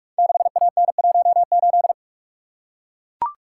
Morse code, heard as a keyed tone, about 44 wpm, 700 Hz. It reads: 6AN18
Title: Code Groups - 5 Letter:Numbers - 45wpm